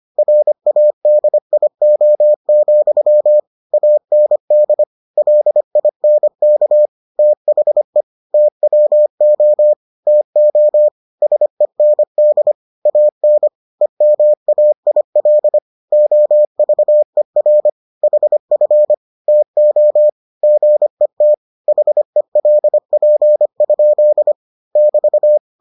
Morse code, heard as a keyed tone, about 25 wpm, 600 Hz. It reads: RADIO, AND LINK THE TWO TO SEND AN EMAIL OVER HF TO GET HELP? <BT>